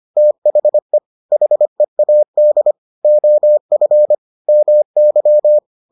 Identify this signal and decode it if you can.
Morse code, heard as a keyed tone, about 25 wpm, 600 Hz. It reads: THE HEAD OF MY